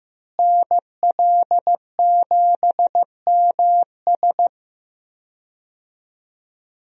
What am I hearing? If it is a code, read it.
Morse code, heard as a keyed tone, about 15 wpm, 700 Hz. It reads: NL7MS
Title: USA Call Signs - 5 Characters - Repeated 3x - 15wpm